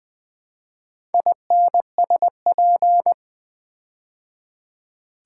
Morse code, heard as a keyed tone, about 21 wpm, 700 Hz. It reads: INSP
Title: Build A Word - 4 Letters At A Time - 11-Letter Words - 20wpm